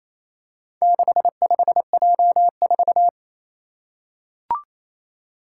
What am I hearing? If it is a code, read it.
Morse code, heard as a keyed tone, about 28 wpm, 700 Hz. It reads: 65J4
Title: Four Challenging Letter-Number Combinations 28wpm